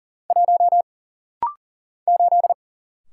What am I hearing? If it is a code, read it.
Morse code, heard as a keyed tone, about 40 wpm, 700 Hz. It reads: 1 8
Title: Single Letter-Number - Rapid-Fire 40wpm